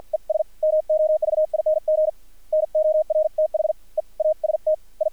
Morse code, heard as a keyed tone, about 55 wpm, spaced as for 43 wpm, 620 Hz, with radio noise. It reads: ES MOVING NORTHEAST I
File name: archived / CRSnet